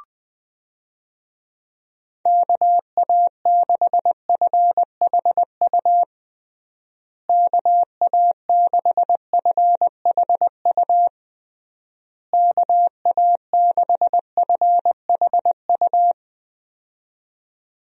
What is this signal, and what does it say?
Morse code, heard as a keyed tone, about 20 wpm, 700 Hz. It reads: KA6FHU KA6FHU KA6FHU
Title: USA Call Signs - 6 Characters - Repeated 3x - 20wpm